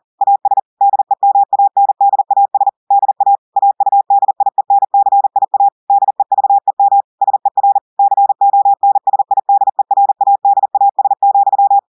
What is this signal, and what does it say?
Morse code, heard as a keyed tone, about 40 wpm, 800 Hz. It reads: AS DEMANDAS DA AUDIENCIA DEVEM SER CONSIDERADAS,
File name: audios / portugais / portugais texte en clair 2 vitesse 40 mots minutes